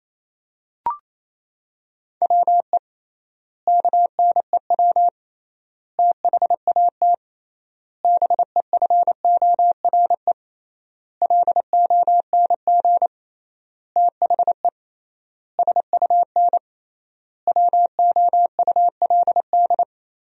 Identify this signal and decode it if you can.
Morse code, heard as a keyed tone, about 28 wpm, 700 Hz. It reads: WE KNEW THAT BEFORE LONG THE SUN WOULD
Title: 3x Word-Spacing - Sentences from Top 400 Words - 28wpm